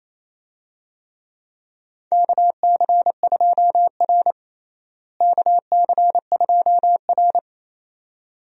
Morse code, heard as a keyed tone, about 28 wpm, 700 Hz. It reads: KC2R KC2R
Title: USA Call Signs - 4 Characters - Repeated 2x - 28wpm